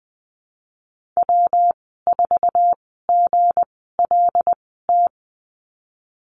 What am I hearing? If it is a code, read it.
Morse code, heard as a keyed tone, about 20 wpm, spaced as for 15 wpm, 700 Hz. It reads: W4GLT
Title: 193 Lesson 31g - Letter v - Call Signs 20wpm